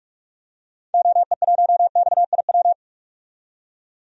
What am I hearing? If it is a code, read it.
Morse code, heard as a keyed tone, about 47 wpm, 700 Hz. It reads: OE1XIW
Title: DXCC-100 Call Signs - 6 Characters - 45wpm